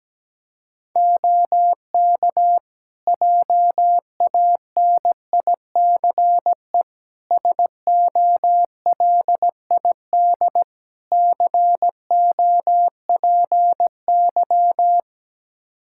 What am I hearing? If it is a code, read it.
Morse code, heard as a keyed tone, about 17 wpm, 700 Hz. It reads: OK JANICE SOLID COPY
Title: K1USN SST QSO Practice 17wpm